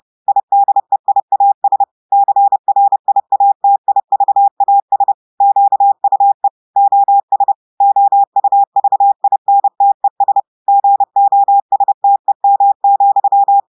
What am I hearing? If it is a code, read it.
Morse code, heard as a keyed tone, about 30 wpm, 800 Hz. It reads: IDEIAS CRIATIVAS QUE OS OUVINTES GOSTEM,
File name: portugais texte en clair 2 vitesse 30 mots minutes